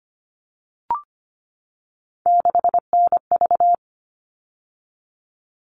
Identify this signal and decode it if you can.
Morse code, heard as a keyed tone, about 25 wpm, 700 Hz. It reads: 6NV
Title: All Cut Numbers - 3-Digit 25wpm